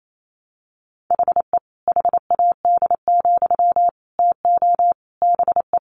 Morse code, heard as a keyed tone, about 28 wpm, 700 Hz. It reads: HE HAD, TO BE
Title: Sentences from Top 500 Words 28wpm